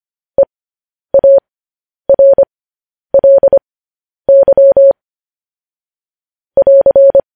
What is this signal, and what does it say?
Morse code, heard as a keyed tone, about 25 wpm, spaced as for 10 wpm, 550 Hz. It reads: EARLY <AR>
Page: news headlines in morse code at 10 wpm